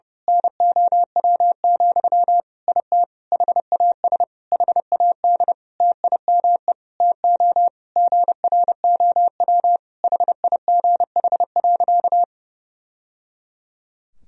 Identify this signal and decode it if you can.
Morse code, heard as a keyed tone, about 32 wpm, 700 Hz. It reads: NOW, IT HAS HAD TIME TO GROW HIGH.
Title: Sentences from Top 1000 Words - Review Part 1 - 30wpm